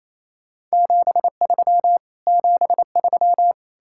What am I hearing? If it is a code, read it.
Morse code, heard as a keyed tone, about 38 wpm, spaced as for 33 wpm, 700 Hz. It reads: 73 73
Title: Speed-Racing Common QSO Phrases - 28wpm